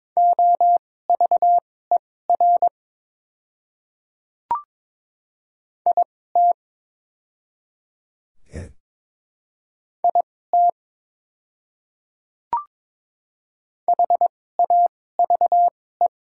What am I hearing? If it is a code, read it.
Morse code, heard as a keyed tone, about 22 wpm, spaced as for 16 wpm, 690 Hz. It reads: OVER IT IT HAVE
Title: 2x Character Spacing - Top 100 Words - 22wpm